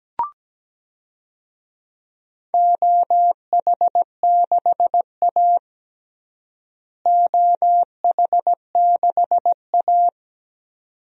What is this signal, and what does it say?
Morse code, heard as a keyed tone, about 17 wpm, 700 Hz. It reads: OH6A OH6A
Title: DXCC-100 Call Signs - 4 Characters - Repeated 2x - 17wpm